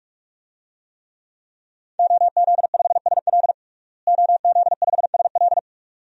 Morse code, heard as a keyed tone, about 42 wpm, 700 Hz. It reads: OZ5SL OZ5SL
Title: DXCC-100 Call Signs - 5 Characters - Repeated 2x - 45wpm